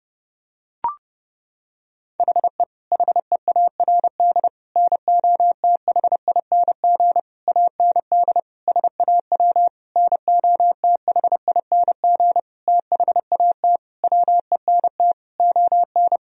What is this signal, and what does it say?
Morse code, heard as a keyed tone, about 30 wpm, 700 Hz. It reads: HE HEARD NOTHING AND SAW NOTHING THAT WENT ON
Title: Sentences from Top 1000 Words - Review Part 3 - 30wpm